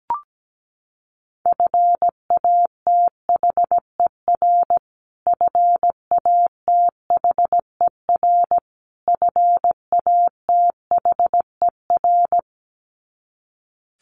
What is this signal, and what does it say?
Morse code, heard as a keyed tone, about 17 wpm, 700 Hz. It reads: FATHER FATHER FATHER
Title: Top 500 Words Repeated 17wpm